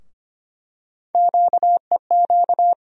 Morse code, heard as a keyed tone, about 25 wpm, 700 Hz. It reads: QEQ
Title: USA Call Sign Postfixes - 3 Letters 25wpm